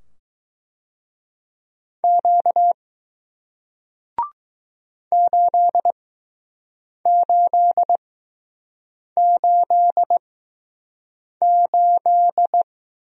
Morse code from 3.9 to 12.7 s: 8888

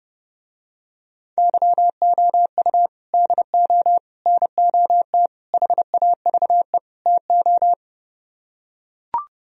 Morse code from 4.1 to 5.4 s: NOT

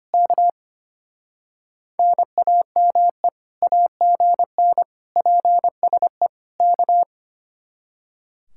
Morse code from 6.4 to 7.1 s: K

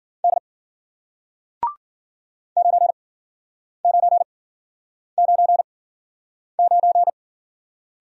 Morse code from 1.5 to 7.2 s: 9999

